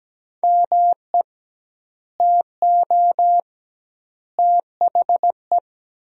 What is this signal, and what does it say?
Morse code, heard as a keyed tone, about 17 wpm, 700 Hz. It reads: ME TO THE
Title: 2x Word-Spacing - Sentences from Top 400 Words - 17wpm